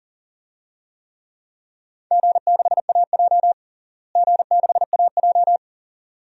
Morse code, heard as a keyed tone, about 40 wpm, 700 Hz. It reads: G6AJ G6AJ
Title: DXCC-100 Call Signs - 4 Characters - Repeated 2x - 40wpm